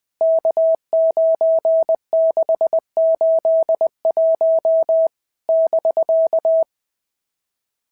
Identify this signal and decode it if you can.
Morse code, heard as a keyed tone, about 20 wpm, 650 Hz. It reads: K9681 <BK>
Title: POTA QSO Practice - 20wpm